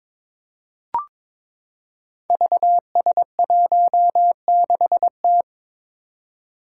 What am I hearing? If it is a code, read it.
Morse code, heard as a keyed tone, about 22 wpm, 700 Hz. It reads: VS16T
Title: Five Challenging Letter-Number Combinations 22wpm